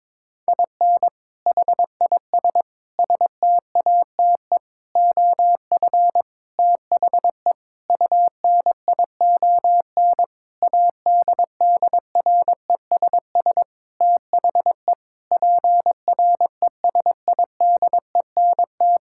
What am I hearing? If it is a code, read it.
Morse code, heard as a keyed tone, about 22 wpm, 700 Hz. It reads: IN HIS STATE OF THE UNION ADDRESS THE PRESIDENT
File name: Sentences with 7-Word Phrases - 22wpm